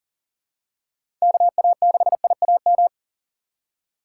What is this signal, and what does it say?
Morse code, heard as a keyed tone, about 40 wpm, 700 Hz. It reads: KA6IAM
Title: USA Call Signs - 6 Characters - 2x Character Spacing - 40wpm